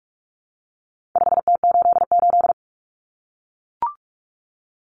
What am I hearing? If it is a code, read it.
Morse code, heard as a keyed tone, about 44 wpm, 700 Hz. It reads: 5T88
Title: Common Cut Numbers - 4-Digit 45wpm